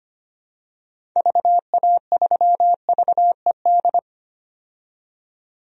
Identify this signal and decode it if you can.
Morse code, heard as a keyed tone, about 25 wpm, 700 Hz. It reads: VA3VED
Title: DXCC-100 Call Signs - 6 Characters - 25wpm